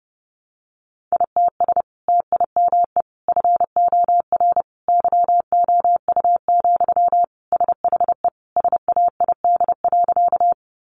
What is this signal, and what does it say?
Morse code, heard as a keyed tone, about 30 wpm, 700 Hz. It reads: ITS TIME FOR YOU, SHE SAID.
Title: Sentences from Top 600 Words 30wpm